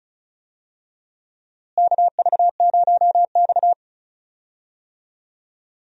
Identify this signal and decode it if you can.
Morse code, heard as a keyed tone, about 34 wpm, 700 Hz. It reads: KV0X
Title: USA Call Signs - 4 Characters - Repeated 2x - 35wpm